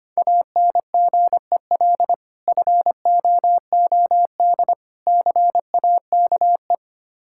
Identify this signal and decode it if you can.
Morse code, heard as a keyed tone, about 25 wpm, 700 Hz. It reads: ANGEL FOOD CAKE
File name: practice / 25wpm / Difficult Words with Hints - 25wpm